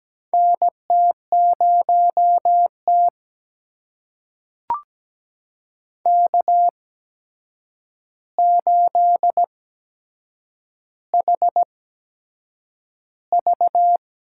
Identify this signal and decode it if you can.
Morse code, heard as a keyed tone, about 17 wpm, spaced as for 7 wpm, 700 Hz. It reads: NT0T K8HV
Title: USA Call Signs - 4 Characters - 8x Character Spacing - 17wpm